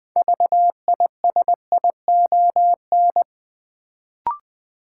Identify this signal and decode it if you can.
Morse code, heard as a keyed tone, about 20 wpm, 700 Hz. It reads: VISION
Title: Build A Word - 2 Letters At A Time - 6-Letter Words - 20wpm